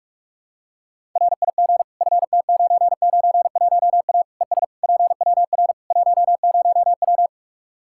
Morse code, heard as a keyed tone, about 46 wpm, 700 Hz. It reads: RIG FT991A ES PWR 10W
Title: Practice QSO - Exchange 2 - Info - 45wpm